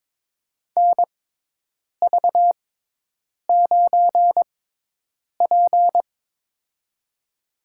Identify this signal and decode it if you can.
Morse code, heard as a keyed tone, about 22 wpm, spaced as for 8 wpm, 700 Hz. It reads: NV9P
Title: USA Call Signs - 4 Characters - 6x Character Spacing - 22wpm